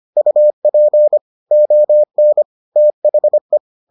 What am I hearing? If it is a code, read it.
Morse code, heard as a keyed tone, about 25 wpm, 600 Hz. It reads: UP ON THE